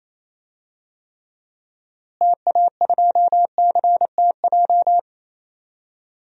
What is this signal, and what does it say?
Morse code, heard as a keyed tone, about 27 wpm, 700 Hz. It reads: TA2CTJ
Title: DXCC-100 Call Signs - 6 Characters - Repeated 2x - 28wpm